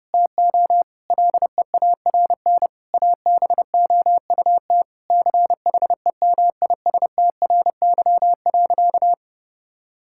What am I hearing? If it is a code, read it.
Morse code, heard as a keyed tone, about 32 wpm, 700 Hz. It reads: TO LEARN ABOUT CHEMISTRY.
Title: Sentences from Top 1000 Words - Review Part 6 - 30wpm